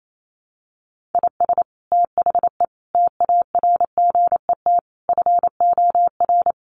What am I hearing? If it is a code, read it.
Morse code, heard as a keyed tone, about 28 wpm, 700 Hz. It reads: IS THE TARGET FOR
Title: Sentences from Top 1200 Words 28wpm